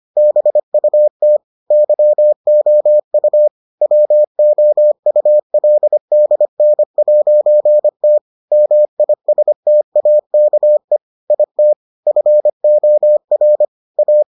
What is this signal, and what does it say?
Morse code, heard as a keyed tone, about 25 wpm, 600 Hz. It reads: BUT YOU WOULDN'T MISTAKE IT FOR A